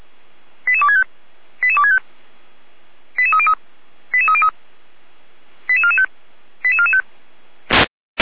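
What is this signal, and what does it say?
Five-tone selective call (ZVEI), heard as quick runs of five tones, tones of about 70 ms. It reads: 88265 88265 88303 88303 88445 88445
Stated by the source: ZVEI five-tone selective call received on SDR.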